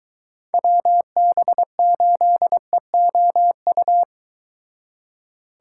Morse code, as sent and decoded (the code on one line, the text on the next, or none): .-- -... ---.. --- ..-
WB8EOU